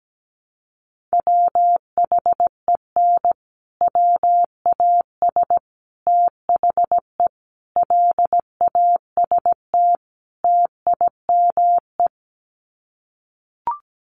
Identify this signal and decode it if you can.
Morse code, heard as a keyed tone, about 17 wpm, 700 Hz. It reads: WHEN WAS THE LAST TIME
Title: Sets of 5 Words Encore 17wpm